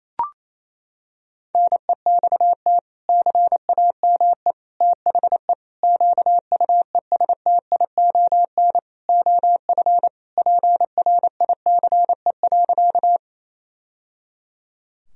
Morse code, heard as a keyed tone, about 28 wpm, 700 Hz. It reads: NEXT CAME THE QUESTION OF PRICE.
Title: Sentences from Top 1000 Words - Review Part 4 - 28wpm